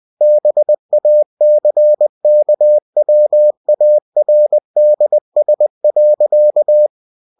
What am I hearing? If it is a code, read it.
Morse code, heard as a keyed tone, about 20 wpm, 600 Hz. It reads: BACKWARDS.
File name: cw-books / winnie-the-pooh / winnie-20wpm-page-0076